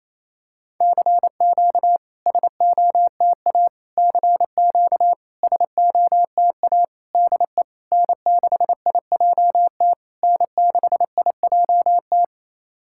Morse code, heard as a keyed tone, about 28 wpm, 690 Hz. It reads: CQ SOTA CQ SOTA DE N6IJT N6IJT
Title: SOTA QSO Practice 28wpm